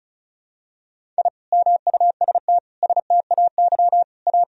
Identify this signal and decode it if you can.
Morse code, heard as a keyed tone, about 36 wpm, 700 Hz. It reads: I MUST STAY A
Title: Sentences from Top 1000 Words - Review Part 1 - 35wpm